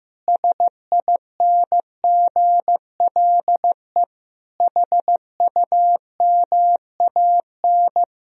Morse code, heard as a keyed tone, about 16 wpm, 700 Hz. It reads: SINGLE HUMAN
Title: Sentences from Top 1000 Words - Review Part 4 - 15wpm